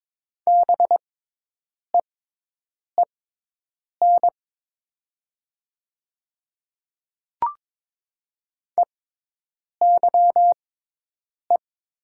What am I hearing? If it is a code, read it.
Morse code, heard as a keyed tone, about 22 wpm, spaced as for 8 wpm, 700 Hz. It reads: BEEN EYE
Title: 6x Character Spacing - Top 300 Words - 22wpm